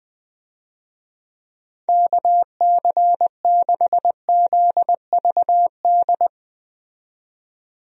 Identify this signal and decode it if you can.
Morse code, heard as a keyed tone, about 20 wpm, 700 Hz. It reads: KC6ZVD
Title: USA Call Signs - 6 Characters - 20wpm